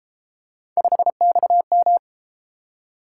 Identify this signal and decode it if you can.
Morse code, heard as a keyed tone, about 30 wpm, 700 Hz. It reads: 5XM